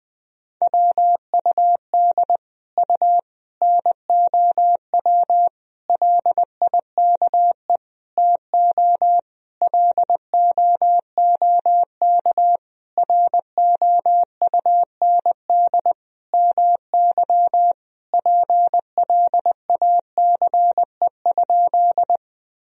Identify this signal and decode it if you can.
Morse code, heard as a keyed tone, about 20 wpm, 700 Hz. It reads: WUD U NOW LIKE TO LOOK ROUND MY PLACE?
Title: Sentences from Top 200 Words and Abbreviations 20wpm